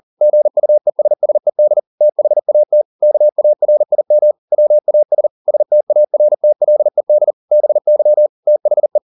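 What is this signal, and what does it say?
Morse code, heard as a keyed tone, about 40 wpm, 600 Hz. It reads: GUESSED THAT KARIM WAS STARTLED BY THE
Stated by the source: Morse code at 40 wpm, practice text, CW med Gnister.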